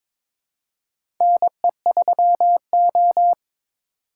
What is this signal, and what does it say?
Morse code, heard as a keyed tone, about 22 wpm, 700 Hz. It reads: NE3O
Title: USA Call Signs - 4 Characters - 3x Character Spacing - 22wpm